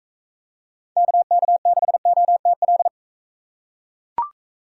Morse code, heard as a keyed tone, about 42 wpm, 700 Hz. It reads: KK6OTL